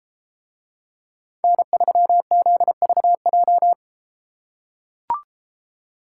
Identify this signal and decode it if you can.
Morse code, heard as a keyed tone, about 33 wpm, 700 Hz. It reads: N3ZVJ